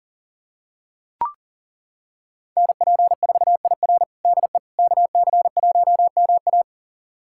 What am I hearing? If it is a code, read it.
Morse code, heard as a keyed tone, about 40 wpm, 700 Hz. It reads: NP4IR DE KC1MA
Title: USA Callsign DE Callsign - 5 Characters - 40wpm